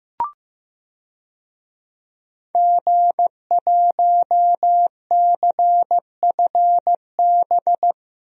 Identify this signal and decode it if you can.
Morse code, heard as a keyed tone, about 15 wpm, 700 Hz. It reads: G1CFB
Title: DXCC-100 Call Signs - 5 Characters - Repeated 2x - 15wpm